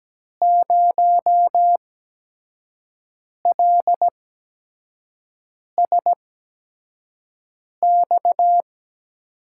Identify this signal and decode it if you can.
Morse code, heard as a keyed tone, about 17 wpm, spaced as for 5 wpm, 700 Hz. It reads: 0LSX
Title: USA Call Signs - 6 Characters - 8x Character Spacing - 17wpm